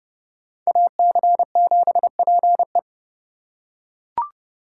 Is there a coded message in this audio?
Yes: Morse code from 0.7 to 2.8 s, heard as a keyed tone, about 30 wpm, 700 Hz. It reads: AC7PE